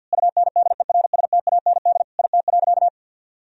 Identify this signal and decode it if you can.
Morse code, heard as a keyed tone, about 52 wpm, 700 Hz. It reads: UNDERSTAND IT.
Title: Sentences from Top 1000 Words - Review Part 4 - 50wpm